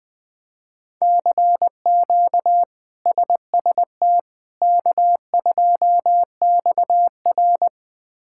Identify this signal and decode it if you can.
Morse code, heard as a keyed tone, about 20 wpm, 700 Hz. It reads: CQ SST K2XR
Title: K1USN SST QSO Practice - 20wpm